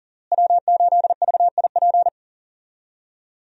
Morse code, heard as a keyed tone, about 40 wpm, 700 Hz. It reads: W8VIP
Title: USA Call Signs - 5 Characters - Repeated 2x - 40wpm